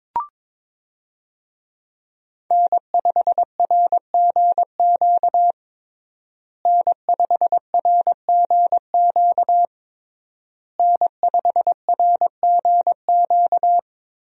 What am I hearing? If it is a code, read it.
Morse code, heard as a keyed tone, about 22 wpm, 700 Hz. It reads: N5RGQ N5RGQ N5RGQ
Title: USA Call Signs - 5 Characters - Repeated 3x - 22wpm